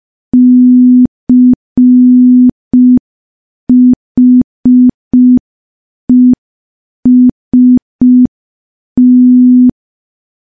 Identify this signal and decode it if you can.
Morse code, heard as a keyed tone, about 5 wpm, 250 Hz. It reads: CHEST
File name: morse / morse-training-05WPM